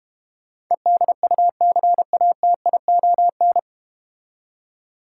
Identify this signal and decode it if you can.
Morse code, heard as a keyed tone, about 30 wpm, 690 Hz. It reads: EDUCATION
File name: Easy Words - Speed-Racing with Hints - 25wpm